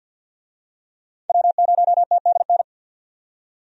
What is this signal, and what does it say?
Morse code, heard as a keyed tone, about 52 wpm, 700 Hz. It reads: W0TDN
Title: USA Call Signs - 5 Characters - 50wpm